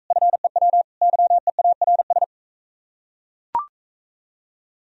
Morse code, heard as a keyed tone, about 40 wpm, 700 Hz. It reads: FEW YEARS